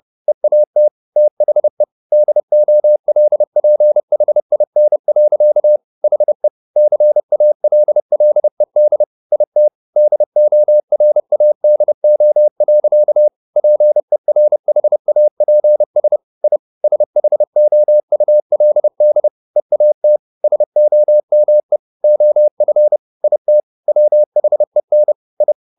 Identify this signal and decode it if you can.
Morse code, heard as a keyed tone, about 30 wpm, 600 Hz. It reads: EAT THE DOLPHIN. HE CALLED IT DORADO. PERHAPS I SHOULD EAT SOME OF IT WHEN I